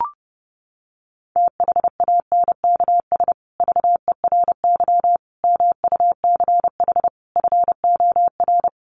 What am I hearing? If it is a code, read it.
Morse code, heard as a keyed tone, about 30 wpm, 700 Hz. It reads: THANKS VERY MUCH FOR
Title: Sets of 5 Words Encore2 30wpm